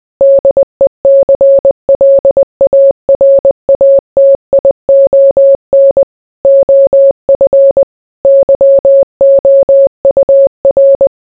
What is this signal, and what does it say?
Morse code, heard as a keyed tone, about 20 wpm, 550 Hz. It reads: DECLARATION OF YOUR